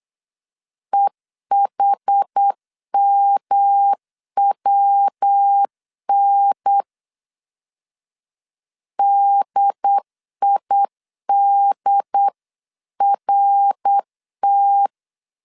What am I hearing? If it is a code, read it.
Morse code, heard as a keyed tone, about 8 wpm, 800 Hz. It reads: EHMWN DIDRT